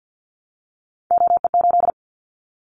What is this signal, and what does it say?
Morse code, heard as a keyed tone, about 50 wpm, 690 Hz. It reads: OE8
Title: Callsign Prefixes for Top 100 DXCC - 50wpm